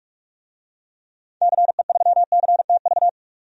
Morse code, heard as a keyed tone, about 46 wpm, 700 Hz. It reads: CE3CTV
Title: DXCC-100 Call Signs - 6 Characters - 45wpm